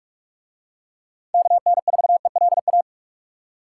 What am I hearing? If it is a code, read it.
Morse code, heard as a keyed tone, about 43 wpm, 700 Hz. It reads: KN4ELA